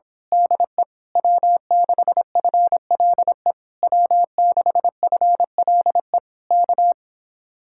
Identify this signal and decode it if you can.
Morse code, heard as a keyed tone, about 26 wpm, 700 Hz. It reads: DE W6FLE W6FLE K
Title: Speed-Racing Common QSO Phrases Encore - 22wpm